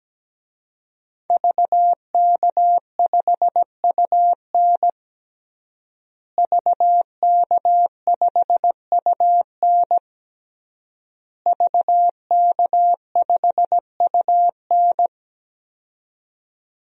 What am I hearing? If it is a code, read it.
Morse code, heard as a keyed tone, about 17 wpm, 700 Hz. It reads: VK5UN VK5UN VK5UN
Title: DXCC-100 Call Signs - 5 Characters - Repeated 3x - 17wpm